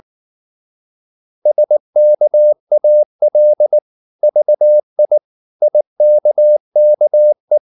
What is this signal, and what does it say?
Morse code, heard as a keyed tone, about 19 wpm, 600 Hz. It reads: SKAL VI IKKE
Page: Ferie 19 wpm